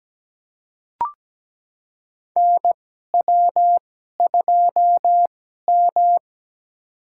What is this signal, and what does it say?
Morse code, heard as a keyed tone, about 17 wpm, spaced as for 12 wpm, 700 Hz. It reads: NW2M
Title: USA Call Signs - 4 Characters - 2x Character Spacing - 17wpm